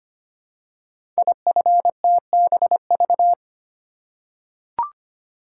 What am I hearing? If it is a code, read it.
Morse code, heard as a keyed tone, about 25 wpm, 700 Hz. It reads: IFTBV
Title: Code Groups - 5 Letters - 25wpm